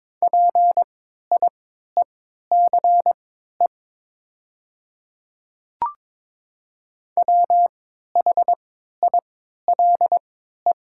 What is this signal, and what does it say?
Morse code, heard as a keyed tone, about 22 wpm, spaced as for 12 wpm, 700 Hz. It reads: PIECE WHILE
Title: 3x Character Spacing - Top 500 Words - 22wpm